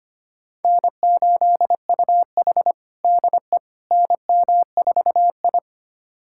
Morse code, heard as a keyed tone, about 25 wpm, 700 Hz. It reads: N8UH DE NM4I
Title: USA Callsign DE Callsign - 4 Characters - 25wpm